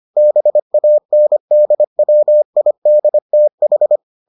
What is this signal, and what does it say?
Morse code, heard as a keyed tone, about 25 wpm, 600 Hz. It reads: BANDWIDTH